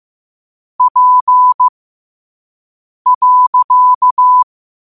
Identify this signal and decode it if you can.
Morse code, heard as a keyed tone, about 15 wpm, spaced as for 5 wpm, 1000 Hz. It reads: P.
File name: qotdfiles / 5wpm